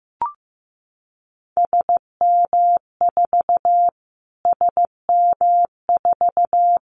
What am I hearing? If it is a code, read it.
Morse code, heard as a keyed tone, about 15 wpm, 700 Hz. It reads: SM4 SM4
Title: Callsign Prefixes for Top 25 DXCC - Repeated 3x - 15wpm